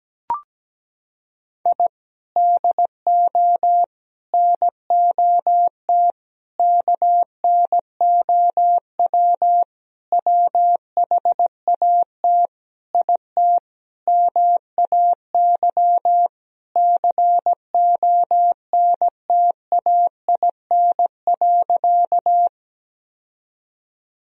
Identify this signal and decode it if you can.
Morse code, heard as a keyed tone, about 17 wpm, 700 Hz. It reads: I DO NOT KNOW WHAT IT MAY CONTAIN.
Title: Sentences from Top 1000 Words - Review Part 2 - 17wpm